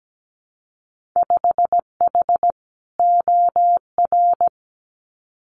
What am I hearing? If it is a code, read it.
Morse code, heard as a keyed tone, about 17 wpm, 700 Hz. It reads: 5H OR
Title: Winter Field Day Exchanges Encore 17wpm